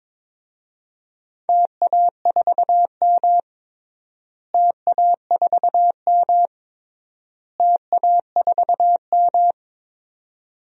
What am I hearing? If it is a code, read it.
Morse code, heard as a keyed tone, about 22 wpm, 700 Hz. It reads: TA4M TA4M TA4M
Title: DXCC-100 Call Signs - 4 Characters - Repeated 3x - 22wpm